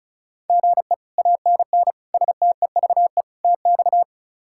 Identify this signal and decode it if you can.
Morse code, heard as a keyed tone, about 36 wpm, 700 Hz. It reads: GE ANN STEVE TX
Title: K1USN SST QSO Practice - 35wpm